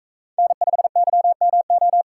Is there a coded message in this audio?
Yes: Morse code from 0.4 to 2.0 s, heard as a keyed tone, about 39 wpm, 700 Hz. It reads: N5YMO